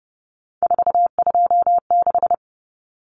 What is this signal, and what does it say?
Morse code, heard as a keyed tone, about 30 wpm, 700 Hz. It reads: 426
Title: Common Cut Numbers - 3-Digit 30wpm